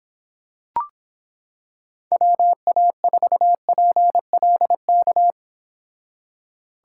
Morse code, heard as a keyed tone, about 26 wpm, 700 Hz. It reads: WA4PLK